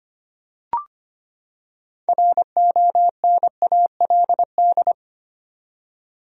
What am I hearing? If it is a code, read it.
Morse code, heard as a keyed tone, about 25 wpm, 700 Hz. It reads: RONALD
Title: USA Common Names - 25wpm